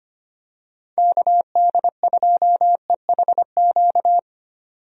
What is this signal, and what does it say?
Morse code, heard as a keyed tone, about 25 wpm, 700 Hz. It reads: KD2EHQ
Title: USA Call Signs - 6 Characters - 25wpm